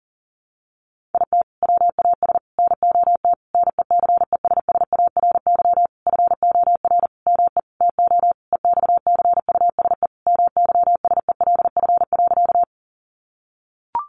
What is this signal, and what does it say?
Morse code, heard as a keyed tone, about 40 wpm, 700 Hz. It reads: IT WAS NOT NECESSARY FOR ME TO EXCUSE MYSELF.
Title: Sentences from Top 1300 Words 40wpm